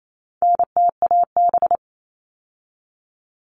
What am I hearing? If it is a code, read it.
Morse code, heard as a keyed tone, about 28 wpm, 700 Hz. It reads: NTAB
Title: All Cut Numbers - 4-Digit 28wpm